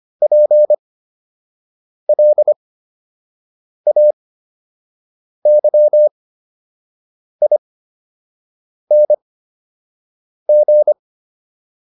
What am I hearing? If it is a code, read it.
Morse code, heard as a keyed tone, about 25 wpm, spaced as for 6 wpm, 600 Hz. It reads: PLAYING